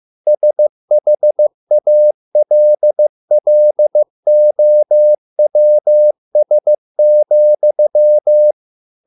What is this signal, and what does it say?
Morse code, heard as a keyed tone, about 15 wpm, 600 Hz. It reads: SHALLOWS,